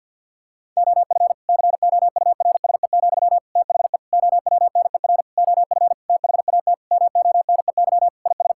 Morse code, heard as a keyed tone, about 50 wpm, 700 Hz. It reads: OF COURSE, THE OWNER OF THAT MONEY IS
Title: Sentences from Top 1600 Words 50wpm